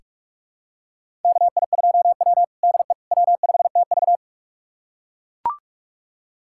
Morse code, heard as a keyed tone, about 41 wpm, 700 Hz. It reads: KI2W DE W5TV
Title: USA Callsign DE Callsign - 4 Characters - 45wpm